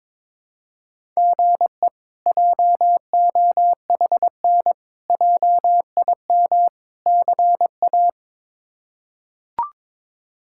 Morse code, heard as a keyed tone, about 22 wpm, 700 Hz. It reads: GE JOHN JIM CA
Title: K1USN SST QSO Practice - 22wpm